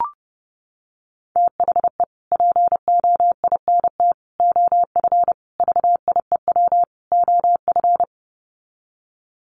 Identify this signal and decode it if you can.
Morse code, heard as a keyed tone, about 30 wpm, 700 Hz. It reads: THE POINT OF VIEW OF
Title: Sets of 5 Words 30wpm